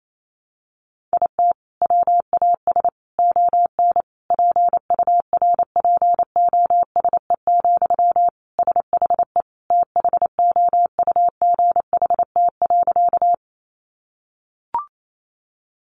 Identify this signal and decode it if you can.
Morse code, heard as a keyed tone, about 28 wpm, 700 Hz. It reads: IT WAS ON PURPOSE, SHE THOUGHT.
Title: Sentences from Top 800 Words 28wpm